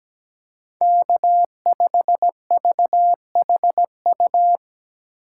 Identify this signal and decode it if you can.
Morse code, heard as a keyed tone, about 17 wpm, 700 Hz. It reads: K5VHU
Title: USA Call Signs - 5 Characters - 2x Character Spacing - 17wpm